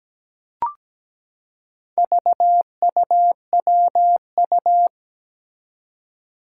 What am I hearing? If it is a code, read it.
Morse code, heard as a keyed tone, about 17 wpm, 700 Hz. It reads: VUWU
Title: Four Challenging Letter-Number Combinations 17wpm